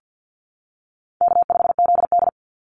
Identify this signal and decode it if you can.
Morse code, heard as a keyed tone, about 50 wpm, 700 Hz. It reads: K5ZD
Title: Easy Call Signs - 50wpm